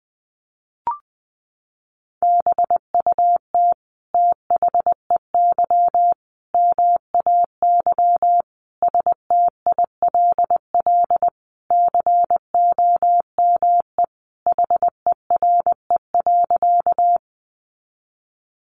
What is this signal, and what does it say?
Morse code, heard as a keyed tone, about 20 wpm, 700 Hz. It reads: BUT THEY MAY STILL COME HERE.
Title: Sentences from Top 300 Words 20wpm